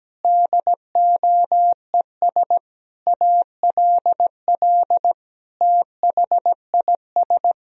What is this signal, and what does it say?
Morse code, heard as a keyed tone, about 17 wpm, 700 Hz. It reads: DOES ALL THIS
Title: Sentences from Top 1000 Words - Review Part 2 - 17wpm